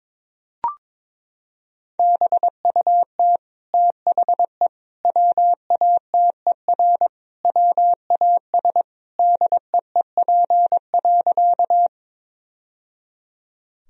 Morse code, heard as a keyed tone, about 22 wpm, 700 Hz. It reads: BUT THE WATER WAS DEEP.
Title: Sentences from Top 1000 Words - Review Part 3 - 22wpm